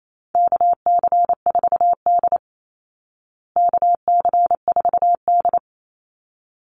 Morse code, heard as a keyed tone, about 28 wpm, 700 Hz. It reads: KC4D KC4D
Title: Call Signs Repeated 3x - 28wpm